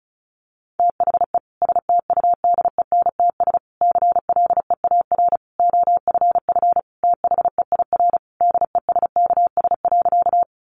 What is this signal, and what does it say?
Morse code, heard as a keyed tone, about 34 wpm, 700 Hz. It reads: THE STUDENTS CLEAR OFF THEIR DESKS.
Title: Additional Easy Phrases 35wpm